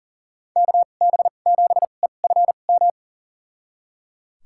Morse code, heard as a keyed tone, about 38 wpm, spaced as for 28 wpm, 700 Hz. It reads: KB7EFM
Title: USA Call Signs - 6 Characters - 2x Character Spacing - 40wpm